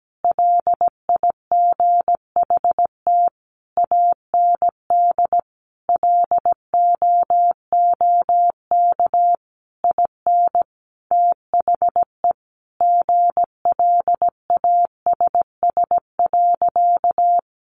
Morse code, heard as a keyed tone, about 17 wpm, 700 Hz. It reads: LIGHT AND LOOK IN THE GLASS.
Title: Sentences from Top 1300 Words 17wpm